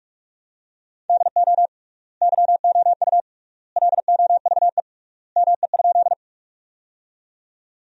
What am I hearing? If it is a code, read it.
Morse code, heard as a keyed tone, about 47 wpm, 700 Hz. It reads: DO YOU LOVE ME?
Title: 3x Word-Spacing - Sentences from Top 400 Words - 45wpm